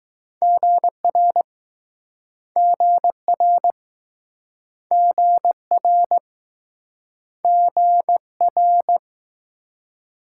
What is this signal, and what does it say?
Morse code, heard as a keyed tone, about 19 wpm, 700 Hz. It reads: GR GR GR GR